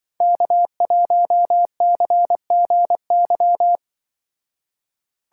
Morse code, heard as a keyed tone, about 24 wpm, 700 Hz. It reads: K1CGY
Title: Speed Racing - USA 1x3 Call Signs 20wpm